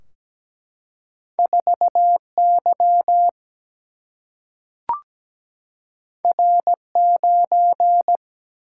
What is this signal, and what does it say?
Morse code, heard as a keyed tone, about 17 wpm, 700 Hz. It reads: 4 Y R 9
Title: Single-Letter Single-Number 17wpm